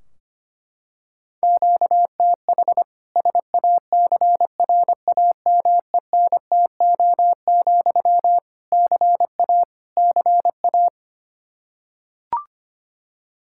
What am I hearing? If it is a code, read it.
Morse code, heard as a keyed tone, about 25 wpm, 700 Hz. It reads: QTH SACRAMENTO, CA CA
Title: K1USN SST QSO Practice 25wpm